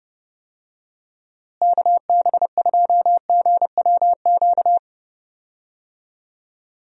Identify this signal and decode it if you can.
Morse code, heard as a keyed tone, about 30 wpm, 700 Hz. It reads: KB2GWQ
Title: USA Call Signs - 6 Characters - Repeated 3x - 30wpm